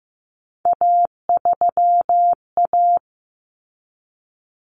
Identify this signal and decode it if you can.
Morse code, heard as a keyed tone, about 15 wpm, 700 Hz. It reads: A3A
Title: Common Cut Numbers - 3-Digit 15wpm